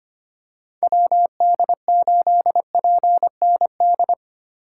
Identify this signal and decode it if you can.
Morse code, heard as a keyed tone, about 25 wpm, 690 Hz. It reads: WD8PND
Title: USA Call Signs - 4x Character Spacing - 25wpm